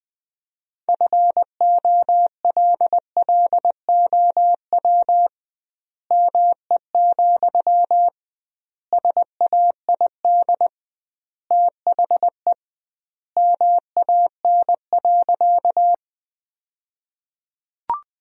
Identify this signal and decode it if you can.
Morse code, heard as a keyed tone, about 21 wpm, 700 Hz. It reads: FOLLOW ME, SAID THE MAN.
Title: 2x Word-Spacing - Sentences from Top 200 Words - 20wpm